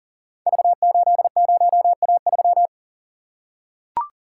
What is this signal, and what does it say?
Morse code, heard as a keyed tone, about 43 wpm, 700 Hz. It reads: V80A3
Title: Code Groups - 5 Letter:Numbers - 40wpm